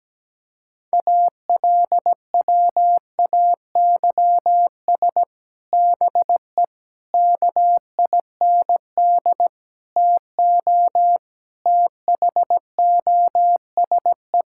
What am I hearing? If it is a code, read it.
Morse code, heard as a keyed tone, about 17 wpm, 700 Hz. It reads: ALWAYS BE KIND TO THOSE
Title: Sentences from Top 1600 Words 17wpm